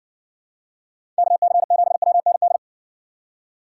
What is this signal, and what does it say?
Morse code, heard as a keyed tone, about 60 wpm, 700 Hz. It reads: KC7WND